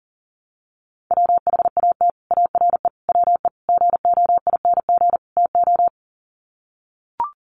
Morse code, heard as a keyed tone, about 40 wpm, 700 Hz. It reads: WHAT ARE WE GOING TO
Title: Sets of 5 Words 40wpm